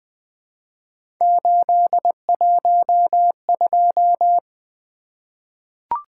Morse code, heard as a keyed tone, about 20 wpm, 700 Hz. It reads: 812
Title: Three Letter-Number Combinations - 20wpm